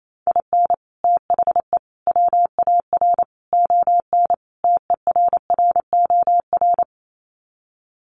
Morse code, heard as a keyed tone, about 28 wpm, 700 Hz. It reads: IN THE WAR ON TERROR
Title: Sets of 5 Words Encore2 28wpm